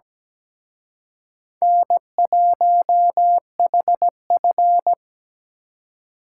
Morse code, heard as a keyed tone, about 20 wpm, 700 Hz. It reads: N1HF
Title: Speed Racing - USA 1x2 Call Signs 17wpm